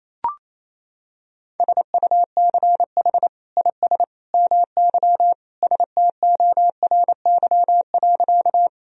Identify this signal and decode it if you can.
Morse code, heard as a keyed tone, about 28 wpm, 700 Hz. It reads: SUCH IS MY STORY.
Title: Sentences from Top 1000 Words - Review Part 1 - 28wpm